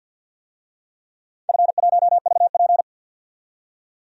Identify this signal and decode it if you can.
Morse code, heard as a keyed tone, about 47 wpm, 700 Hz. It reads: F1VP
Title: DXCC-100 Call Signs - 4 Characters - 50wpm